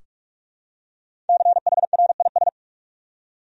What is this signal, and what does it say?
Morse code, heard as a keyed tone, about 43 wpm, 700 Hz. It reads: CHRIS
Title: USA Common Names - Max 5 Letters - 45wpm